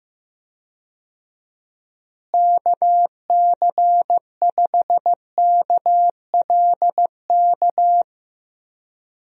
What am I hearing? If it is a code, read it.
Morse code, heard as a keyed tone, about 15 wpm, 700 Hz. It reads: KC5KLK
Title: USA Call Signs - 6 Characters - Repeated 2x - 15wpm